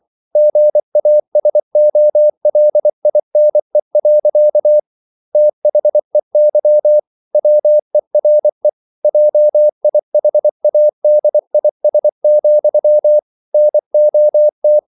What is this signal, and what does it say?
Morse code, heard as a keyed tone, about 24 wpm, 600 Hz. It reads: GASOLINE. THEY WERE JIHADIS, NOT